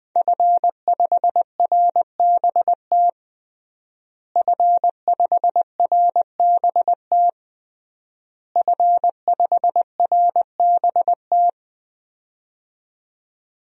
Morse code, heard as a keyed tone, about 20 wpm, 700 Hz. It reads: F5RBT F5RBT F5RBT
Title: DXCC-100 Call Signs - 5 Characters - Repeated 3x - 20wpm